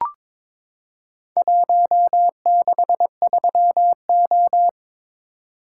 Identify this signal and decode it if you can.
Morse code, heard as a keyed tone, about 22 wpm, 700 Hz. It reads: 163O
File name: practice / 22wpm / Four Letter-Number Combinations - 22wpm